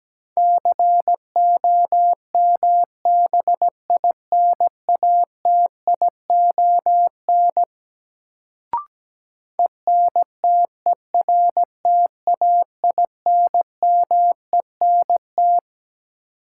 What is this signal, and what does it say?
Morse code, heard as a keyed tone, about 17 wpm, 700 Hz. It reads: COMBINATION ENTERTAINMENT
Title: Sentences from Top 1000 Words - Review Part 6 - New Word Drill - 17wpm